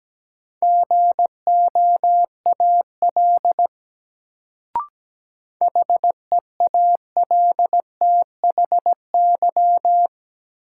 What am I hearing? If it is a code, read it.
Morse code, heard as a keyed tone, about 17 wpm, 700 Hz. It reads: GOAL HEALTHY
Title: Sentences from Top 1400 Words - New Word Drill Alt - 17wpm